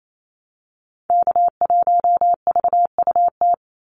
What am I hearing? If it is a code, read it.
Morse code, heard as a keyed tone, about 28 wpm, 700 Hz. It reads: K1VUT
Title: Call Signs Repeated 2x - 28wpm